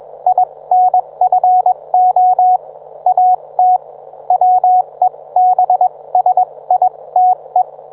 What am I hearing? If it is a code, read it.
Morse code, heard as a keyed tone, about 21 wpm, spaced as for 18 wpm, 730 Hz. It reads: INFO AT WEBSITE
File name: book